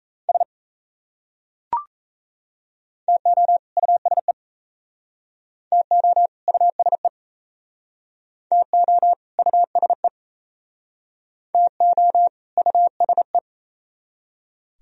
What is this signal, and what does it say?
Morse code, heard as a keyed tone, about 37 wpm, 700 Hz. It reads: S TO USE TO USE TO USE TO USE